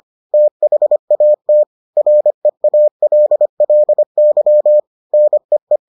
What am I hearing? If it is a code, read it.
Morse code, heard as a keyed tone, about 25 wpm, 600 Hz. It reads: THAT REALLY NEE